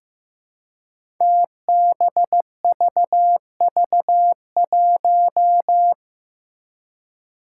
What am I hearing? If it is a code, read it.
Morse code, heard as a keyed tone, about 15 wpm, 700 Hz. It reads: TBVV1
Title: Five Challenging Letter-Number Combinations 15wpm